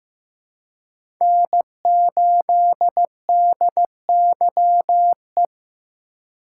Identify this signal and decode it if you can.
Morse code, heard as a keyed tone, about 15 wpm, 700 Hz. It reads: N8DYE
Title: USA Call Signs - 5 Characters - 2x Character Spacing - 15wpm